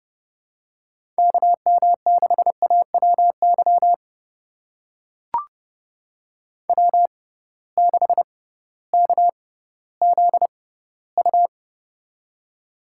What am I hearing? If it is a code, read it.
Morse code, heard as a keyed tone, about 30 wpm, spaced as for 16 wpm, 700 Hz. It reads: KM6AWY W6KZU
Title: USA Call Signs - 6x Character Spacing - 30wpm